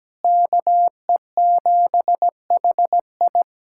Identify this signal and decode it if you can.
Morse code, heard as a keyed tone, about 17 wpm, 700 Hz. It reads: KE7HI
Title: USA Callsign DE Callsign - 5 Characters - 17wpm